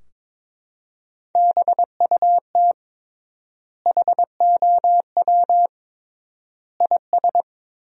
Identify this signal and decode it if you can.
Morse code, heard as a keyed tone, about 22 wpm, 700 Hz. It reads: BUT HOW IS
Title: 3x Word-Spacing - Sentences from Top 100 Words - 22wpm